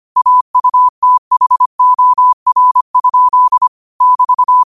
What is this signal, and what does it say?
Morse code, heard as a keyed tone, about 25 wpm, 1000 Hz. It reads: AUTHOR? <BT>
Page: Quotes for Wed, 13 Aug 2025 in Morse Code at 25 words per minute.